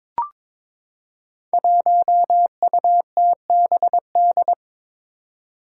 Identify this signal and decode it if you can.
Morse code, heard as a keyed tone, about 21 wpm, 700 Hz. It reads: 1UTBD
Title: Five Challenging Letter-Number Combinations 22wpm